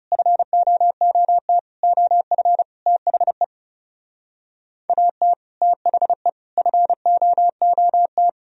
Morse code, heard as a keyed tone, about 38 wpm, 700 Hz. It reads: FOOT OF THE AT THE FOOT